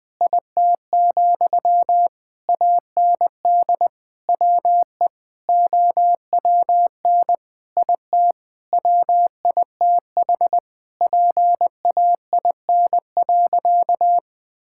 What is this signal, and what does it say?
Morse code, heard as a keyed tone, about 20 wpm, 700 Hz. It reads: IT, AND WE OWN IT WITH PAIN.
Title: Sentences from Top 1600 Words 20wpm